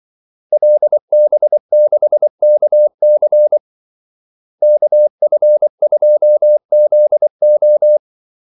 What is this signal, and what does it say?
Morse code, heard as a keyed tone, about 24 wpm, 600 Hz. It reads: LB6KC KF2ZO
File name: Callsigns-24wpm